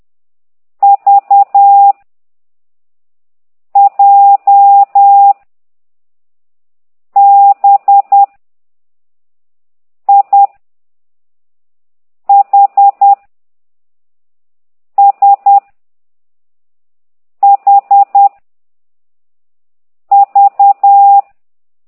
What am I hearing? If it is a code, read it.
Morse code, heard as a keyed tone, about 10 wpm, 800 Hz. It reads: V J B I H S H V
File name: ct2idw / exerc6